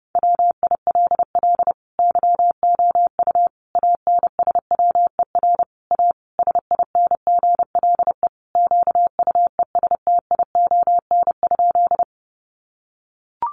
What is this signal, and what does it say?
Morse code, heard as a keyed tone, about 30 wpm, 700 Hz. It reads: WILL YOU ANSWER A SINGLE QUESTION?
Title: Sentences from Top 700 Words 30wpm